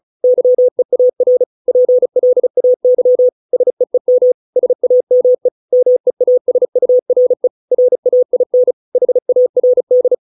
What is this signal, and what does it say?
Morse code, heard as a keyed tone, about 34 wpm, 500 Hz. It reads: YEAR PLAY SEEM SAME MEASURE RAIN HARD